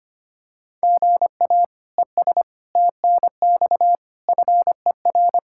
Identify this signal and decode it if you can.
Morse code, heard as a keyed tone, about 25 wpm, 700 Hz. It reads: GA ES TNX FER
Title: K1USN SST QSO Practice 25wpm